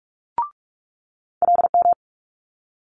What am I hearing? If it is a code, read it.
Morse code, heard as a keyed tone, about 46 wpm, 700 Hz. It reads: LM
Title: Multi-letter Phonemes Quick Fire 45wpm